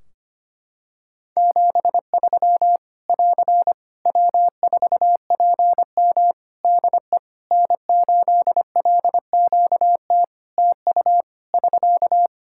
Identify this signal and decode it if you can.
Morse code, heard as a keyed tone, about 25 wpm, 700 Hz. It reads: 73 <AR> W4PM DE N8LQT TU <SK>
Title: Practice QSO - Exchange 4 - Ending - 25wpm